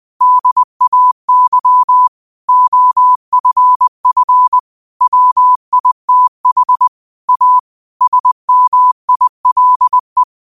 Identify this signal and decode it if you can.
Morse code, heard as a keyed tone, about 20 wpm, 1000 Hz. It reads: DAY OFF WITH A SMILE
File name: 20wpm